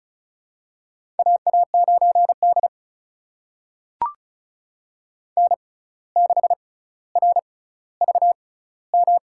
Morse code, heard as a keyed tone, about 36 wpm, spaced as for 17 wpm, 700 Hz. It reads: AA9D N6RVM
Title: USA Call Signs - 6x Character Spacing - 35wpm